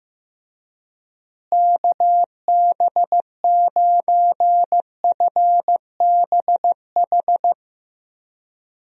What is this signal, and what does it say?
Morse code, heard as a keyed tone, about 16 wpm, 700 Hz. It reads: KB9FBH
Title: USA Call Signs - 6 Characters - Repeated 2x - 15wpm